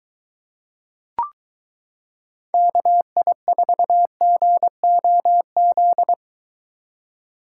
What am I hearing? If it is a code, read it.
Morse code, heard as a keyed tone, about 22 wpm, 700 Hz. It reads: KI4GOZ